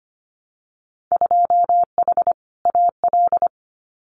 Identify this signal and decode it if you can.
Morse code, heard as a keyed tone, about 25 wpm, 690 Hz. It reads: 2H AL
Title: Winter Field Day Exchanges 25wpm